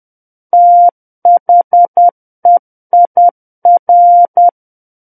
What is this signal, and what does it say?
Morse code, heard as a keyed tone, about 10 wpm, 700 Hz. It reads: THEIR
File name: Book One / 01-Chapter 1 - 10 WPM